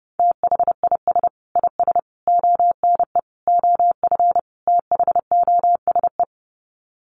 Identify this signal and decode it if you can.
Morse code, heard as a keyed tone, about 30 wpm, 700 Hz. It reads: THIS IS ONE OF THOSE
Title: Sets of 5 Words Encore2 30wpm